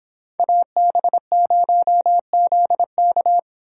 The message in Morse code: AB0ZK

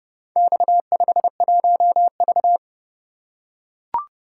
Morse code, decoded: X51V

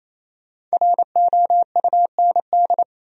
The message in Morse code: ROUND

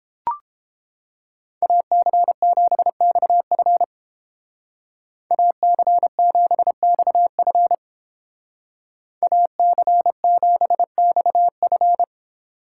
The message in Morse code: AC7XF AC7XF AC7XF

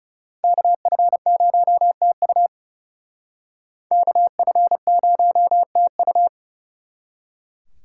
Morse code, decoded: KF0TU KF0TU

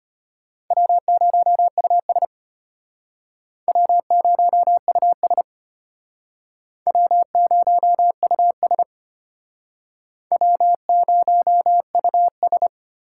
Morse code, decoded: W0US W0US W0US W0US

